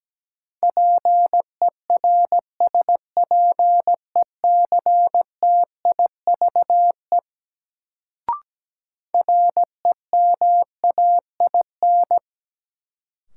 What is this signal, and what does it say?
Morse code, heard as a keyed tone, about 17 wpm, 700 Hz. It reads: PERSPECTIVE REMAIN
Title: Sentences from Top 1200 Words - New Word Drill - 17wpm